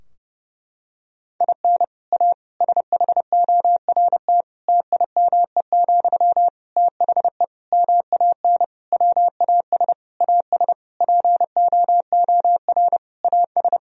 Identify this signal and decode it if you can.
Morse code, heard as a keyed tone, about 30 wpm, 700 Hz. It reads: IN A SHORT TIME, THE MAN WAS AS POOR AS
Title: Sentences from Top 1000 Words - Review Part 2 - 30wpm